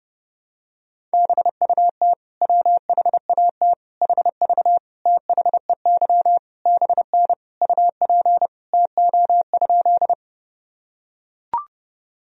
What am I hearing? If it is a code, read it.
Morse code, heard as a keyed tone, about 32 wpm, 700 Hz. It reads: BUT WHAT HV THEY BN UP TO?
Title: Sentences from Top 100 Words and Abbreviations 30wpm